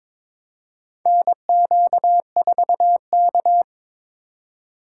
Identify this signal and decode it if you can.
Morse code, heard as a keyed tone, about 22 wpm, 700 Hz. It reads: NQ4K
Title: USA Call Signs - 4 Characters - 4x Character Spacing - 22wpm